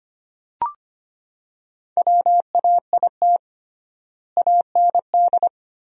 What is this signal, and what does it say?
Morse code, heard as a keyed tone, about 25 wpm, 700 Hz. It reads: WAIT AND
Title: 3x Word-Spacing - Sentences from Top 500 Words - 25wpm